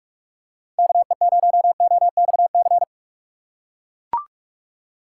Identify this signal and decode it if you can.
Morse code, heard as a keyed tone, about 43 wpm, 700 Hz. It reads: KE0OXC
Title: USA Call Signs - 6 Characters - 2x Character Spacing - 45wpm